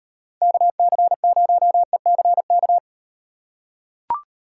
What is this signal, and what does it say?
Morse code, heard as a keyed tone, about 36 wpm, 700 Hz. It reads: KC0ECK